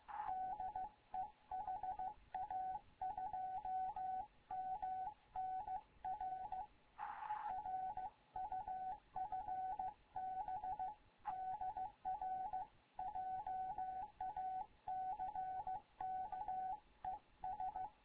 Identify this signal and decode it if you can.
Morse code, heard as a keyed tone, about 15 wpm, 750 Hz. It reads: DE HA2MN R R UFB DR JACK ES
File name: Morze-HA2MN / hangp / 2 11 GYAKORLAT